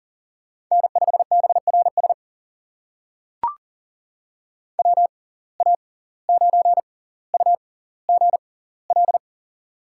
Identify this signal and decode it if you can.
Morse code, heard as a keyed tone, about 40 wpm, spaced as for 19 wpm, 700 Hz. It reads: N5BRS WA9UGL
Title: USA Call Signs - 6x Character Spacing - 40wpm